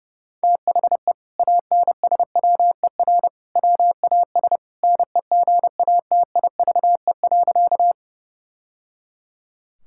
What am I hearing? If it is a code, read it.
Morse code, heard as a keyed tone, about 30 wpm, 700 Hz. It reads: THE ANSWER WAS NEGATIVE.
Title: Sentences from Top 1000 Words - Review Part 6 - 30wpm